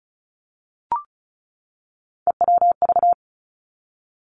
Morse code, heard as a keyed tone, about 36 wpm, 700 Hz. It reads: EWV
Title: Three Challenging Letter-Number Combinations 35wpm